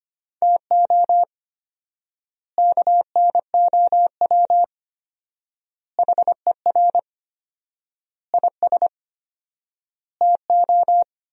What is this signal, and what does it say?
Morse code, heard as a keyed tone, about 25 wpm, 700 Hz. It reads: TO KNOW HER IS TO
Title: 4x Word-Spacing - Sentences from Top 400 Words - 25wpm